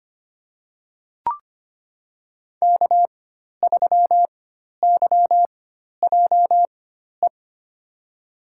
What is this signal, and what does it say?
Morse code, heard as a keyed tone, about 25 wpm, spaced as for 12 wpm, 700 Hz. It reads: K3YJE
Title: USA Call Signs - 5 Characters - 4x Character Spacing - 25wpm